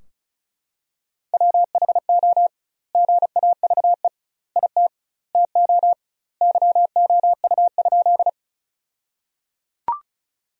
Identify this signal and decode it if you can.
Morse code, heard as a keyed tone, about 36 wpm, 700 Hz. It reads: WHO GAVE IT TO YOU?
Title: 2x Word-Spacing - Sentences from Top 500 Words - 35wpm